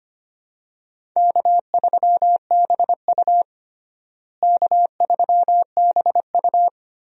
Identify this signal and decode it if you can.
Morse code, heard as a keyed tone, about 25 wpm, 700 Hz. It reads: K3BU K3BU
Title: USA Call Signs - 4 Characters - Repeated 3x - 25wpm